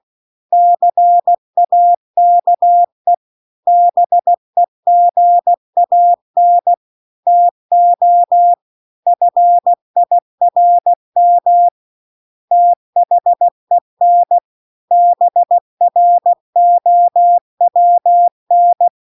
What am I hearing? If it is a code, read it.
Morse code, heard as a keyed tone, about 16 wpm, 700 Hz. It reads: CAKE BEGAN TO FIRM THEN BROWN
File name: CW-The-Big-Twoheaded-River-2-16WPM